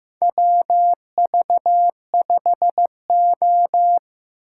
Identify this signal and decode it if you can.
Morse code, heard as a keyed tone, about 15 wpm, 700 Hz. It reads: WV5O
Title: USA Call Signs - 6x Character Spacing - 15wpm